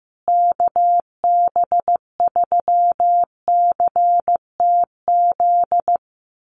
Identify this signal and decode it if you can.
Morse code, heard as a keyed tone, about 15 wpm, 700 Hz. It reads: KB3CTZ
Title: USA Call Signs Repeated 3x 15wpm